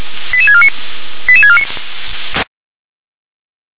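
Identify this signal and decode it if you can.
Five-tone selective call (ZVEI), heard as quick runs of five tones, tones of about 70 ms. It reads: 88649 88649
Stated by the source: ZVEI five-tone selective call received on SDR.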